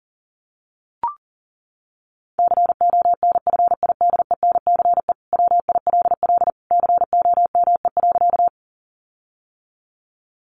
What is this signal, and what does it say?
Morse code, heard as a keyed tone, about 40 wpm, 700 Hz. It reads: CONFIDENCE WILL COME.
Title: Sentences from Top 1500 Words 40wpm